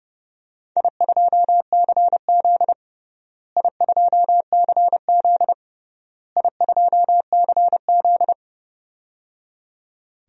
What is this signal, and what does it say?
Morse code, heard as a keyed tone, about 30 wpm, 700 Hz. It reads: I2CZ I2CZ I2CZ
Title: DXCC-100 Call Signs - 4 Characters - Repeated 3x - 30wpm